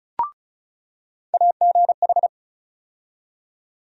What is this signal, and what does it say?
Morse code, heard as a keyed tone, about 36 wpm, 700 Hz. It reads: AGH
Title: USA Call Sign Postfixes - 3 Letters 35wpm